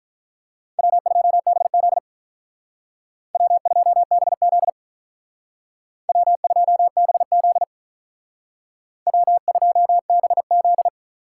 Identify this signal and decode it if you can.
Morse code, heard as a keyed tone, about 46 wpm, 700 Hz. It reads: W2BZ W2BZ W2BZ W2BZ